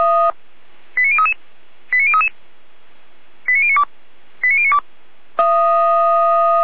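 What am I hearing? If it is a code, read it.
Five-tone selective call (ZVEI), heard as quick runs of five tones, tones of about 70 ms. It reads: 89033 89033 89092 89092
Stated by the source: ZVEI five-tone call received on SDR.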